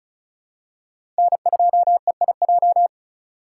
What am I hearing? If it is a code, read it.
Morse code, heard as a keyed tone, about 36 wpm, 700 Hz. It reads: N2EIJ
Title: USA Call Signs - 5 Characters - 3x Character Spacing - 35wpm